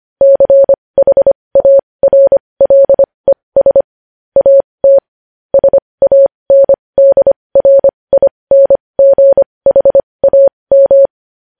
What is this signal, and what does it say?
Morse code, heard as a keyed tone, about 25 wpm, spaced as for 20 wpm, 550 Hz. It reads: CHARLES AT SANDRINGHAM
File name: podcast / News headlines-1x-25x20